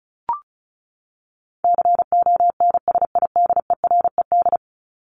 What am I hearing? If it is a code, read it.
Morse code, heard as a keyed tone, about 34 wpm, 700 Hz. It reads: CONSIDERED
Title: ED Suffix - Top 100 Words with Base Words 35wpm